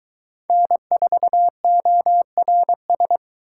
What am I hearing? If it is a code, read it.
Morse code, heard as a keyed tone, about 22 wpm, 700 Hz. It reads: N4ORS